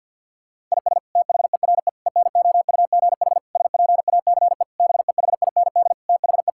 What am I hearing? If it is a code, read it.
Morse code, heard as a keyed tone, about 52 wpm, 690 Hz. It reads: IS THERE ENOUGH SPACE BEHIND THE
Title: Sentences from Top 1000 Words - Review Part 5 - 50wpm